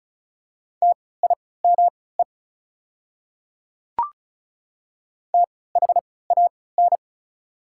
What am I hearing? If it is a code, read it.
Morse code, heard as a keyed tone, about 34 wpm, spaced as for 20 wpm, 700 Hz. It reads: TIME THAN
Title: 3x Character Spacing - Top 300 Words - 35wpm